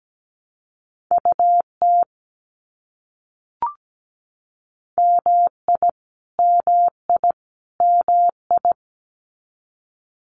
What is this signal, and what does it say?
Morse code, heard as a keyed tone, about 17 wpm, 700 Hz. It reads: UT MI MI MI
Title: US States Repeated 3x 17wpm